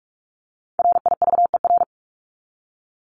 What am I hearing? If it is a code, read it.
Morse code, heard as a keyed tone, about 44 wpm, 700 Hz. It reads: RIVER
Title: Top 500 Words 45wpm